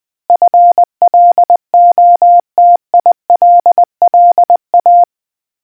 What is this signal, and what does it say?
Morse code, heard as a keyed tone, about 20 wpm, 700 Hz. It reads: FLOTILLA